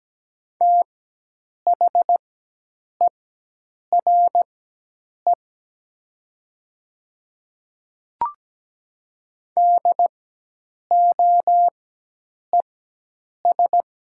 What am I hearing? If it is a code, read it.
Morse code, heard as a keyed tone, about 17 wpm, spaced as for 8 wpm, 700 Hz. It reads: THERE DOES
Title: 4x Character Spacing - Top 300 Words - 17wpm